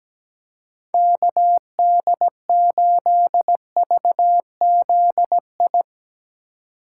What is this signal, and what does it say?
Morse code, heard as a keyed tone, about 17 wpm, 700 Hz. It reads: KD8VZI
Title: Speed Racing - USA 2x3 Call Signs 17wpm